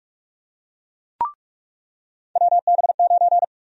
Morse code, heard as a keyed tone, about 42 wpm, 700 Hz. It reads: WB9
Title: USA Call Sign Prefixes - 45wpm